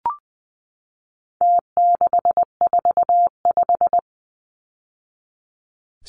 Morse code, heard as a keyed tone, about 20 wpm, 700 Hz. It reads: T645
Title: Numbers - Contest Style - 4-Digit 20wpm